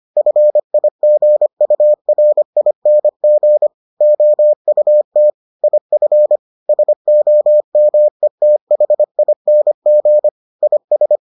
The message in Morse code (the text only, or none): FIGURING OUT IF SOMETHING IS